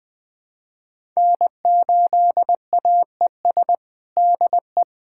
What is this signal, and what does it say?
Morse code, heard as a keyed tone, about 20 wpm, 700 Hz. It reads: N8AES DE
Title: USA Callsign DE Callsign - 5 Characters - 20wpm